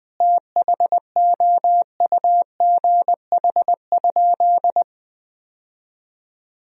Morse code, heard as a keyed tone, about 20 wpm, 700 Hz. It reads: THOUGH?
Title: 3x Word-Spacing - Sentences from Top 400 Words - 20wpm